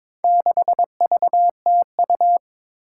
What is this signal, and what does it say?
Morse code, heard as a keyed tone, about 22 wpm, 700 Hz. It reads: 6VTU
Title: Four Challenging Letter-Number Combinations 22wpm